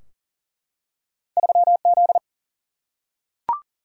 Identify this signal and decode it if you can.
Morse code, heard as a keyed tone, about 40 wpm, 700 Hz. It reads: 3Z
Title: Single-Letter Single-Number 40wpm